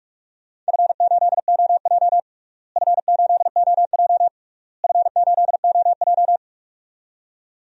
Morse code, heard as a keyed tone, about 41 wpm, 700 Hz. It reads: F8OJ F8OJ F8OJ
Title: DXCC-100 Call Signs - 4 Characters - Repeated 3x - 45wpm